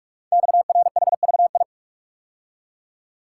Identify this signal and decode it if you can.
Morse code, heard as a keyed tone, about 44 wpm, 700 Hz. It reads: XRHVI
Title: Code Groups - 5 Letters - 45wpm